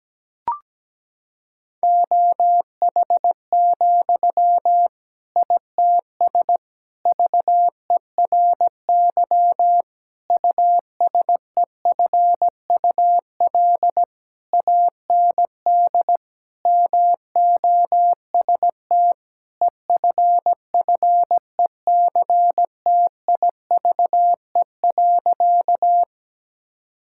Morse code, heard as a keyed tone, about 17 wpm, 700 Hz. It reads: OH, ITS VERY USEFUL AND MOST EFFECTIVE.
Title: Sentences from Top 1000 Words - Review Part 6 - 17wpm